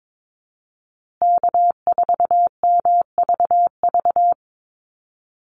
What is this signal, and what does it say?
Morse code, heard as a keyed tone, about 22 wpm, 700 Hz. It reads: K4MVV
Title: USA Call Signs Repeated 3x 22wpm